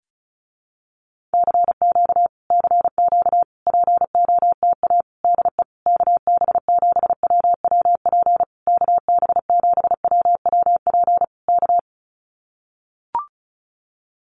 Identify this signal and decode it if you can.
Morse code, heard as a keyed tone, about 34 wpm, 700 Hz. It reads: CQ CQ POTA DE KB7WWP KB7WWP K
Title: Common QSO Phrases Encore 35wpm